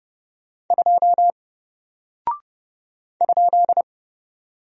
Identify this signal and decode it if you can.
Morse code, heard as a keyed tone, about 30 wpm, 700 Hz. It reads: 2 ?
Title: Single Character (Letter:Number:Punctuation:Prosign) Alt 30wpm